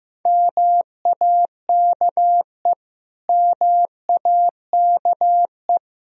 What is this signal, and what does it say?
Morse code, heard as a keyed tone, about 15 wpm, 700 Hz. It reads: MAKE MAKE
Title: Top 100 Words Repeated 15wpm